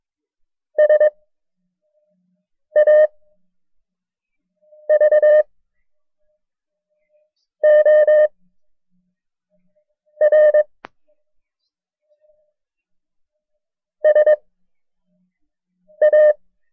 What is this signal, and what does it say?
Morse code, heard as a keyed tone, about 22 wpm, 610 Hz. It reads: S A V O R S A
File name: FreeFiles / CW-practice / KB6MT-tape1-side2